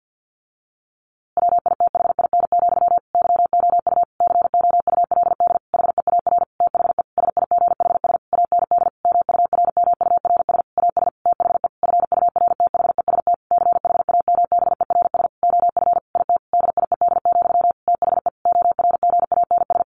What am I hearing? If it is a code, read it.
Morse code, heard as a keyed tone, about 50 wpm, 700 Hz. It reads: WITHIN, YOU COULD HEAR THE SIGHS AND MURMURS AS THE FURTHEST CHAMBERS OF IT DIED, THE ORGANS
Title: A Sound Of Thunder 50wpm